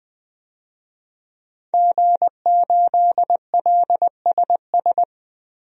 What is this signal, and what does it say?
Morse code, heard as a keyed tone, about 20 wpm, 700 Hz. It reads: G8LSS
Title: DXCC-100 Call Signs - 5 Characters - 20wpm